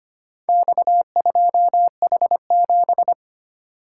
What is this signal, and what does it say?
Morse code, heard as a keyed tone, about 25 wpm, 700 Hz. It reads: X2H7
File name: practice / 25wpm / Four Letter-Number Combinations - 25wpm